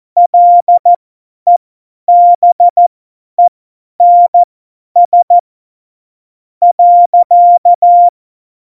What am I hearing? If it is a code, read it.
Morse code, heard as a keyed tone, about 14 wpm, spaced as for 9 wpm, 700 Hz. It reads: LEBENS.
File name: Morsekurs / Witze / Witz 05. Februar 2019, 50 F70